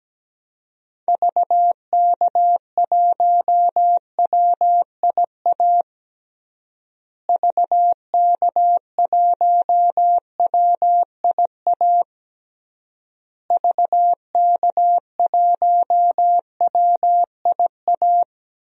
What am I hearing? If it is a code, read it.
Morse code, heard as a keyed tone, about 17 wpm, 700 Hz. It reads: VK1WIA VK1WIA VK1WIA
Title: DXCC-100 Call Signs - 6 Characters - Repeated 3x - 17wpm